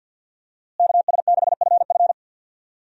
Morse code, heard as a keyed tone, about 47 wpm, 700 Hz. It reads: KS6FF
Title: USA Call Signs - 5 Characters - Repeated 2x - 50wpm